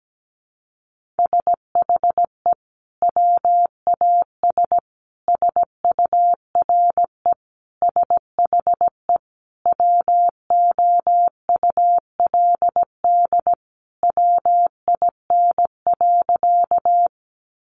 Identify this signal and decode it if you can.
Morse code, heard as a keyed tone, about 17 wpm, 700 Hz. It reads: SHE WAS SURE SHE WOULD WIN.
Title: Sentences from Top 1500 Words 17wpm